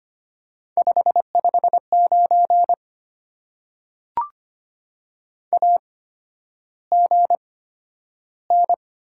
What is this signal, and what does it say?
Morse code, heard as a keyed tone, about 25 wpm, spaced as for 11 wpm, 690 Hz. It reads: 559 AGN
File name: Top 90 QSO Elements - 8x Character Spacing - 25wpm